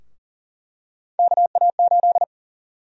Morse code, heard as a keyed tone, about 40 wpm, 700 Hz. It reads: KA8
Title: USA Call Sign Prefixes - 40wpm